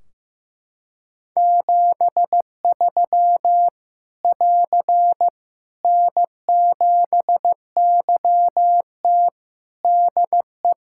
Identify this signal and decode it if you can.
Morse code, heard as a keyed tone, about 15 wpm, 700 Hz. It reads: 73 <AR> N7YT DE
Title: Practice QSO - Exchange 4 - Ending - 15wpm